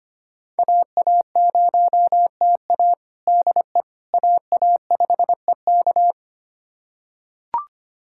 Morse code, heard as a keyed tone, about 25 wpm, 700 Hz. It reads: AA0TA DE AA5EK
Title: USA Callsign DE Callsign - 5 Characters - 25wpm